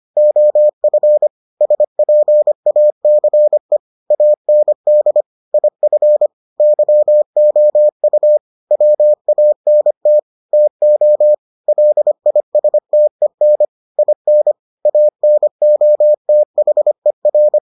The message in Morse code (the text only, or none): OF SPACE AND IF YOU WANT TO LISTEN IN ANOTHER